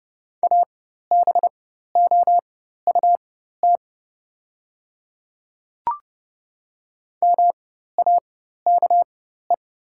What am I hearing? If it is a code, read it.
Morse code, heard as a keyed tone, about 30 wpm, spaced as for 14 wpm, 700 Hz. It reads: ABOUT MAKE
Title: 4x Character Spacing - Top 100 Words - 30wpm